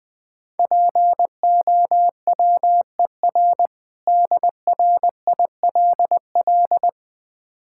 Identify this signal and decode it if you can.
Morse code, heard as a keyed tone, about 20 wpm, 700 Hz. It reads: POWER DRILL
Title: Difficult Words - Speed-Racing with Hints - 20wpm